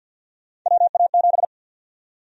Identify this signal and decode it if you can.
Morse code, heard as a keyed tone, about 52 wpm, 700 Hz. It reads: WA7
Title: USA Call Sign Prefixes - 50wpm